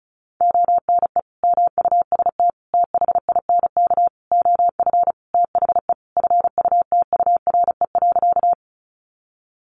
Morse code, heard as a keyed tone, about 34 wpm, 700 Hz. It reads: ONE MUST THINK OF THE FUTURE.
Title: Sentences from Top 700 Words 35wpm